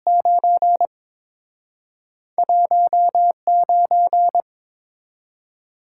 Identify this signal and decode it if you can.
Morse code, heard as a keyed tone, about 29 wpm, 700 Hz. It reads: 9 19